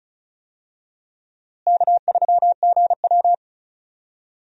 Morse code, heard as a keyed tone, about 36 wpm, 700 Hz. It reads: K3GW
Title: USA Call Signs - 4 Characters - Repeated 3x - 35wpm